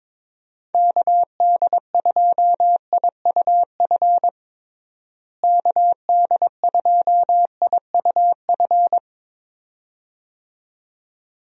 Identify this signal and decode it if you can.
Morse code, heard as a keyed tone, about 22 wpm, 700 Hz. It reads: KD2IUF KD2IUF
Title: USA Call Signs - 6 Characters - Repeated 2x - 22wpm